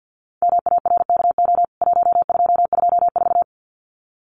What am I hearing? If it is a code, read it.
Morse code, heard as a keyed tone, about 49 wpm, 700 Hz. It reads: MARKO 1224
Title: CWT Exchange 50wpm